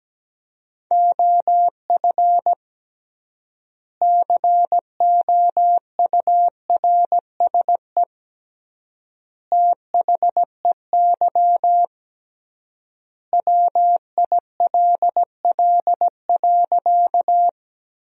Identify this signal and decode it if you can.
Morse code, heard as a keyed tone, about 17 wpm, 700 Hz. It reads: OF COURSE THEY WILL.
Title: 3x Word-Spacing - Sentences from Top 500 Words - 17wpm